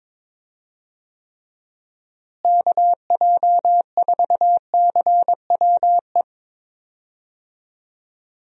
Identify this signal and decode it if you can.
Morse code, heard as a keyed tone, about 22 wpm, 700 Hz. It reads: KJ4CWE
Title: USA Call Signs - 6 Characters - Repeated 2x - 22wpm